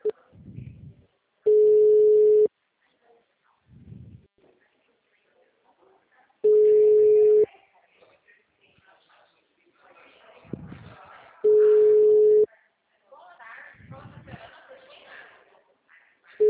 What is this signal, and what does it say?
Ringback tone (Europe, 425 Hz), with speech.